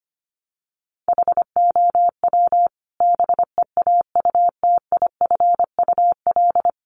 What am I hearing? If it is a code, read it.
Morse code, heard as a keyed tone, about 25 wpm, 700 Hz. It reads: HOW BEAUTIFUL
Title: Sentences from Top 900 Words 25wpm